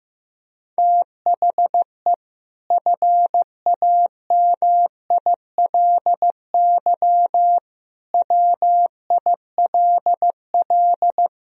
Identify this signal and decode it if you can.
Morse code, heard as a keyed tone, about 15 wpm, 700 Hz. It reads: THE FAMILY WILL
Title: Sentences from Top 1000 Words - Review Part 2 - 15wpm